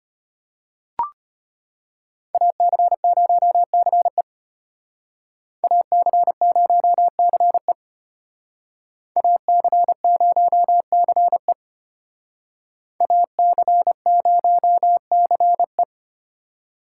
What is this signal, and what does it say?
Morse code, heard as a keyed tone, about 32 wpm, 700 Hz. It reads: AC0CE AC0CE AC0CE AC0CE